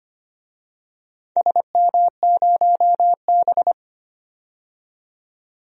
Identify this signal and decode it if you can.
Morse code, heard as a keyed tone, about 25 wpm, 700 Hz. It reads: SM0B
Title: DXCC-100 Call Signs - 4 Characters - 25wpm